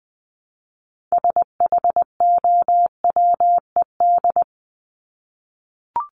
Morse code, heard as a keyed tone, about 20 wpm, 700 Hz. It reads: SHOWED
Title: ED Suffix - Top 100 Words with Base Words 20wpm